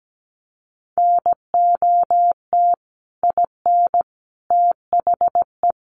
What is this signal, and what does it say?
Morse code, heard as a keyed tone, about 17 wpm, 700 Hz. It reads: NOT IN THE
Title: Sentences from Top 700 Words 17wpm